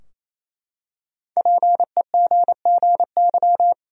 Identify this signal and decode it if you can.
Morse code, heard as a keyed tone, about 28 wpm, 700 Hz. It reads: PEGGY
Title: USA Common Names - Max 5 Letters - 28wpm